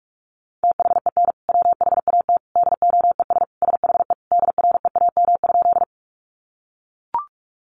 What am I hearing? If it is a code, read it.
Morse code, heard as a keyed tone, about 44 wpm, 700 Hz. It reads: THEN WHAT DOES SHE DREAM?
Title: Sentences from Top 1500 Words 45wpm